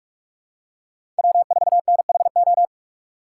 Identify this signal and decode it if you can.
Morse code, heard as a keyed tone, about 46 wpm, 700 Hz. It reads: W4NHO
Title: USA Call Signs - 5 Characters - 6x Character Spacing - 45wpm